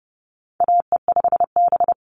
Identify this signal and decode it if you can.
Morse code, heard as a keyed tone, about 30 wpm, 700 Hz. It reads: AE5B
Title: NA SOTA Chaser Callsigns 30wpm